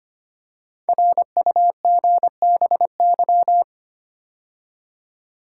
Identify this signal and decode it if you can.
Morse code, heard as a keyed tone, about 25 wpm, 700 Hz. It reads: RUGBY
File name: 5 Letter Words - First 3 Letters Given - 25wpm